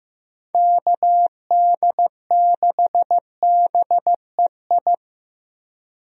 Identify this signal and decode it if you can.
Morse code, heard as a keyed tone, about 15 wpm, 700 Hz. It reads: KD6BEI
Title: USA Call Signs - 6 Characters - Repeated 3x - 15wpm